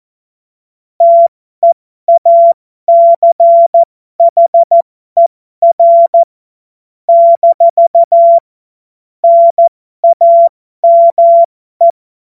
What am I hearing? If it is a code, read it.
Morse code, heard as a keyed tone, about 14 wpm, spaced as for 12 wpm, 680 Hz. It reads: TEACHER - NAME
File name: Morsekurs / Witze